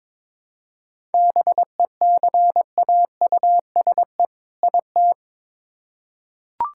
Morse code, heard as a keyed tone, about 22 wpm, 700 Hz. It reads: BECAUSE IT
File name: Build A Phrase - 2-Word Phrases - 22wpm